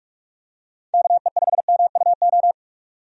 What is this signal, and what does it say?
Morse code, heard as a keyed tone, about 41 wpm, 700 Hz. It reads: KE5MUO
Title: USA Call Signs - 6 Characters - 45wpm